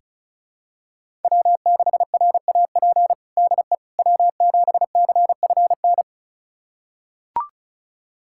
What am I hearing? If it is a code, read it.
Morse code, heard as a keyed tone, about 34 wpm, 700 Hz. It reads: W6RAP DE W7CFN
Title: USA Callsign DE Callsign - 5 Characters - 35wpm